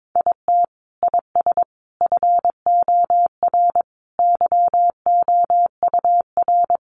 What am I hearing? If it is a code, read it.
Morse code, heard as a keyed tone, about 22 wpm, 700 Hz. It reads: IT IS FOR YOUR
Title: Sentences from Top 700 Words 22wpm